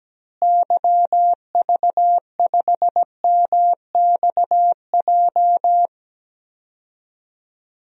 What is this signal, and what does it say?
Morse code, heard as a keyed tone, about 17 wpm, 700 Hz. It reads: YV5MXJ
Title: DXCC-100 Call Signs - 6 Characters - Repeated 2x - 17wpm